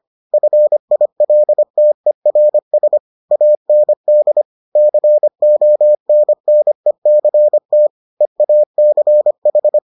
Morse code, heard as a keyed tone, about 25 wpm, 600 Hz. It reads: FILTERS AND CONNECT EACH